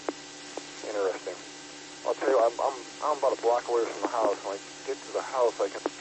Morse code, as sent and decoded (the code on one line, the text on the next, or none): none